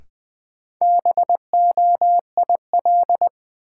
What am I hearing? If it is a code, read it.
Morse code, heard as a keyed tone, about 20 wpm, 700 Hz. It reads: BOIL
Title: Sets of 5 Words Encore - New Word Drill - 20wpm